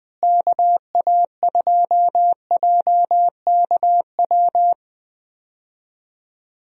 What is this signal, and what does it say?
Morse code, heard as a keyed tone, about 20 wpm, 700 Hz. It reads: KA2JKW
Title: USA Call Signs - 6 Characters - Repeated 3x - 20wpm